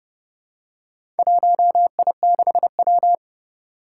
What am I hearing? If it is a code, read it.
Morse code, heard as a keyed tone, about 30 wpm, 700 Hz. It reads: 1I6W
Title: Four Challenging Letter-Number Combinations 30wpm